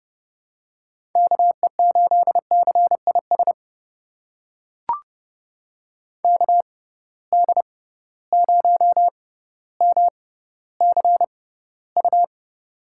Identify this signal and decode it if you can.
Morse code, heard as a keyed tone, about 30 wpm, spaced as for 15 wpm, 700 Hz. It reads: KE8CIS KD0MCU
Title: USA Call Signs - 6 Characters - 6x Character Spacing - 30wpm